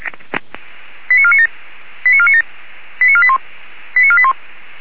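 Five-tone selective call (ZVEI), heard as quick runs of five tones, tones of about 80 ms. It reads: 89497 89497 89481 89481
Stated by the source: Five-tone ZVEI selective call received on an SDR.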